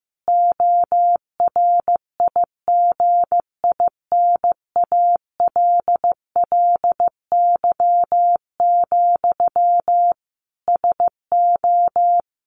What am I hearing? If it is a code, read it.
Morse code, heard as a keyed tone, about 15 wpm, 700 Hz. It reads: ORIGINALLY, SO
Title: A Sound Of Thunder 15wpm